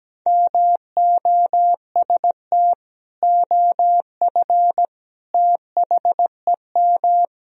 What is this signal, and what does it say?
Morse code, heard as a keyed tone, about 17 wpm, 700 Hz. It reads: MOST OF THEM
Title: Sentences from Top 1000 Words - Review Part 1 - 17wpm